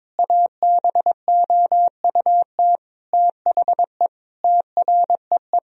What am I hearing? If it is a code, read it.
Morse code, heard as a keyed tone, about 21 wpm, 700 Hz. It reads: ABOUT THE TREE
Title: Sentences from Top 1000 Words - Review Part 1 - 22wpm